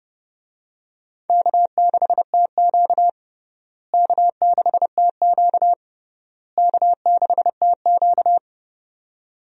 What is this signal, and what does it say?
Morse code, heard as a keyed tone, about 32 wpm, 700 Hz. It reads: K6TQ K6TQ K6TQ
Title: USA Call Signs - 4 Characters - Repeated 3x - 30wpm